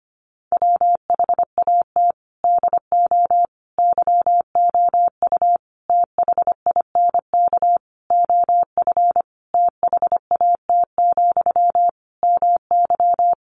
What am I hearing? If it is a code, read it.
Morse code, heard as a keyed tone, about 25 wpm, 700 Hz. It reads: WHAT DO YOU THINK OF THAT, MY
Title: Sentences from Top 1500 Words 25wpm